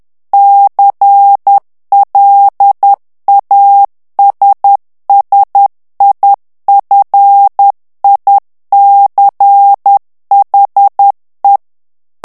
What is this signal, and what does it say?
Morse code, heard as a keyed tone, about 10 wpm, 800 Hz. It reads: CLASSIFICHE
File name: ITALIEN texte clair 2 vitesse 10 mots minute